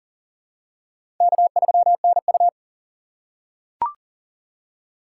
Morse code, heard as a keyed tone, about 39 wpm, 700 Hz. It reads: K3NU
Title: USA Call Signs - 4 Characters - 4x Character Spacing - 40wpm